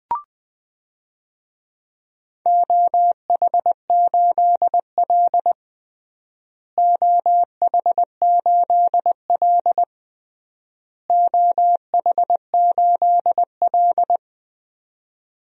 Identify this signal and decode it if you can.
Morse code, heard as a keyed tone, about 20 wpm, 700 Hz. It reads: OH8L OH8L OH8L
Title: DXCC-100 Call Signs - 4 Characters - Repeated 3x - 20wpm